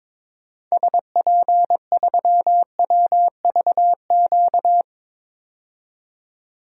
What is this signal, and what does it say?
Morse code, heard as a keyed tone, about 22 wpm, 700 Hz. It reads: SP3WVQ
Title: DXCC-100 Call Signs - 6 Characters - 22wpm